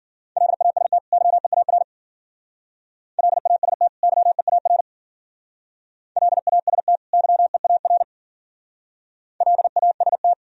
Morse code, heard as a keyed tone, about 50 wpm, 700 Hz. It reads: LAST YEAR LAST YEAR LAST YEAR LAST
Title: Speed-Racing - Sets of 2 Words 40wpm